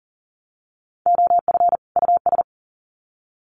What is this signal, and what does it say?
Morse code, heard as a keyed tone, about 40 wpm, 700 Hz. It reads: OF US
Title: Sets of 2 Words 40wpm